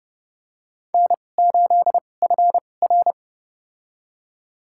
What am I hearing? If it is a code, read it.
Morse code, heard as a keyed tone, about 32 wpm, spaced as for 22 wpm, 700 Hz. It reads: N8FR
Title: USA Call Signs - 4 Characters - 2x Character Spacing - 30wpm